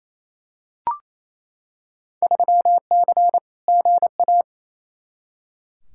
Morse code, heard as a keyed tone, about 28 wpm, 700 Hz. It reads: 3C GA
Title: Field Day Exchanges Encore - 28wpm